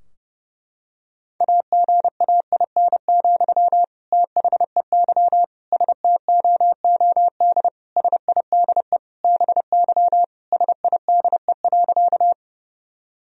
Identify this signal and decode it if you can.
Morse code, heard as a keyed tone, about 30 wpm, 700 Hz. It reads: AGAIN, THEY STOOD SIDE BY SIDE.
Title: Sentences from Top 1000 Words - Review Part 2 - 30wpm